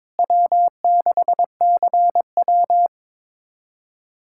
Morse code, heard as a keyed tone, about 22 wpm, 700 Hz. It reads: W6CW
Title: USA Callsign DE Callsign - 4 Characters - 22wpm